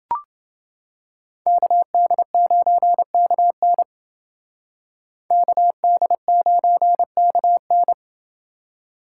Morse code, heard as a keyed tone, about 28 wpm, 700 Hz. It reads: KD9KN KD9KN